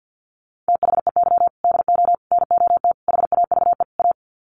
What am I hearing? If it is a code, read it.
Morse code, heard as a keyed tone, about 50 wpm, 700 Hz. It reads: THEY DO NOT HAVE A
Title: Sets of 5 Words 50wpm